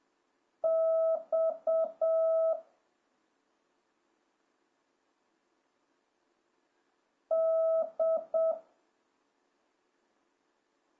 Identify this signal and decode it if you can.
Morse code, heard as a keyed tone, about 7 wpm, 1300 Hz. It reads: X D